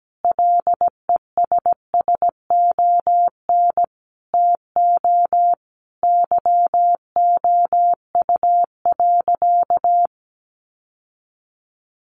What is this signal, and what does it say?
Morse code, heard as a keyed tone, about 17 wpm, 700 Hz. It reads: LESSON TO YOU.
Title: Sentences from Top 1200 Words 17wpm